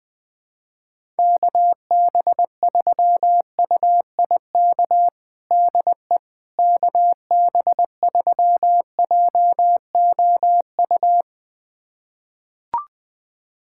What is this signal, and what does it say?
Morse code, heard as a keyed tone, about 21 wpm, 700 Hz. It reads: KB3UIK DE KB3JOU
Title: USA Callsign DE Callsign - 6 Characters - 20wpm